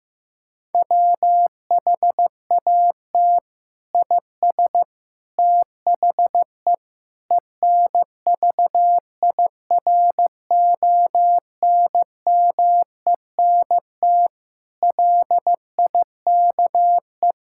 Morse code, heard as a keyed tone, about 15 wpm, 700 Hz. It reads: WHAT IS THE ENVIRONMENT LIKE
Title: Sentences from Top 1000 Words - Review Part 5 - 15wpm